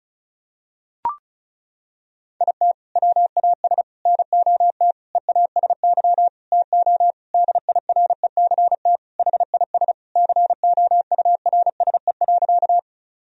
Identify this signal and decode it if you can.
Morse code, heard as a keyed tone, about 34 wpm, 700 Hz. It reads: IT WAS NOT EASY TO DIRECT HIS COURSE.
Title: Sentences from Top 1000 Words - Review Part 2 - 35wpm